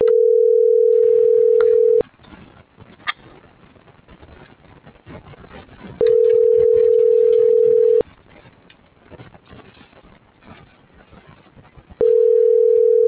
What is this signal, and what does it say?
Ringback tone (North America), 440 + 480 Hz.